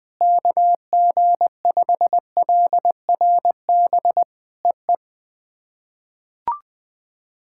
Morse code, heard as a keyed tone, about 20 wpm, 700 Hz. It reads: KG5LRB EE
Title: POTA QSO Practice - 20wpm